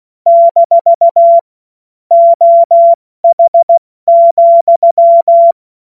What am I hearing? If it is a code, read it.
Morse code, heard as a keyed tone, about 16 wpm, 680 Hz. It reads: - OH,
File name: Witz-26-August-2025-70-F80